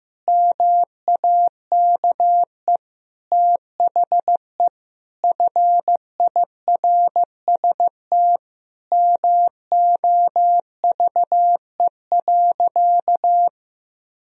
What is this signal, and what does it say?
Morse code, heard as a keyed tone, about 15 wpm, 700 Hz. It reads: MAKE THE FIRST MOVE.
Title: Sentences from Top 1000 Words - Review Part 1 - 15wpm